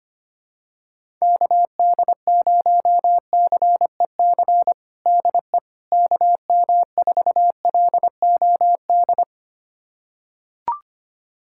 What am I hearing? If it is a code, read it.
Morse code, heard as a keyed tone, about 25 wpm, 700 Hz. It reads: KD0CEC DE KM4LOD
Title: USA Callsign DE Callsign - 6 Characters - 25wpm